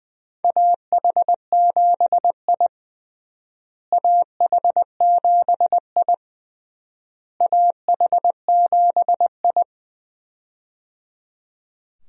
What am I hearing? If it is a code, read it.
Morse code, heard as a keyed tone, about 20 wpm, 700 Hz. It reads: AH7I AH7I AH7I
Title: USA Call Signs - 4 Characters - Repeated 3x - 20wpm